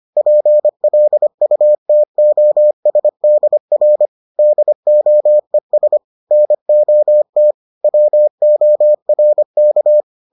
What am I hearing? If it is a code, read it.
Morse code, heard as a keyed tone, about 25 wpm, 600 Hz. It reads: PLUTOSDR DOES NOT WORK